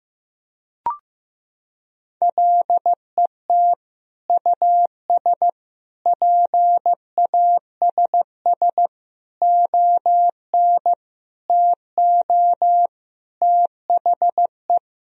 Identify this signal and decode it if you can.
Morse code, heard as a keyed tone, about 15 wpm, 700 Hz. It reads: LET US PASS ON TO THE
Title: Sentences from Top 1000 Words - Review Part 1 - 15wpm